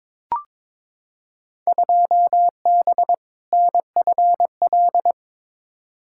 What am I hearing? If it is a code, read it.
Morse code, heard as a keyed tone, about 22 wpm, 700 Hz. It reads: 2B NFL
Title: Field Day Exchanges Encore - 22wpm